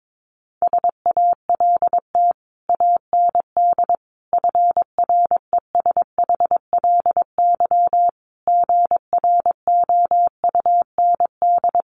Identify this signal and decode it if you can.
Morse code, heard as a keyed tone, about 22 wpm, 700 Hz. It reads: SALT AND FRESHLY GROUND
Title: Sets of 5 Words Encore 22wpm